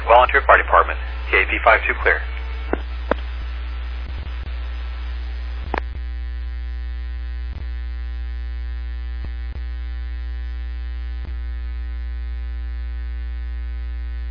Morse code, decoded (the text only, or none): none